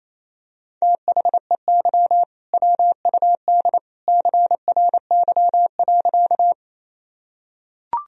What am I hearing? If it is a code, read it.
Morse code, heard as a keyed tone, about 28 wpm, 700 Hz. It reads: THEY WUD CRY.
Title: Sentences from Top 500 Words and Abbreviations 28wpm